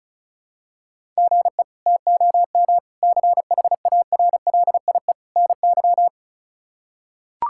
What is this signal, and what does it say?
Morse code, heard as a keyed tone, about 36 wpm, 700 Hz. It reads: GE TOM CHARLIE NY
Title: K1USN SST QSO Practice - 35wpm